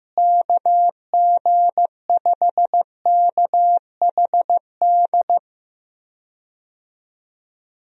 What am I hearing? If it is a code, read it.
Morse code, heard as a keyed tone, about 15 wpm, 700 Hz. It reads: KG5KHD
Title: USA Call Signs - 6 Characters - Repeated 3x - 15wpm